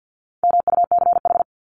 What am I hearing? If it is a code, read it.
Morse code, heard as a keyed tone, about 48 wpm, 700 Hz. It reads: MUCH
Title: Top 300 Words 50wpm